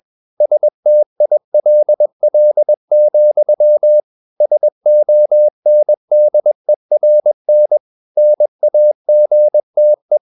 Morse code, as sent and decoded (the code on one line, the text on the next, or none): ... - .. .-.. .-.. --..-- / ... --- -. -.. .-. -. / -. .- --. - .
STILL, SONDERN NAGTE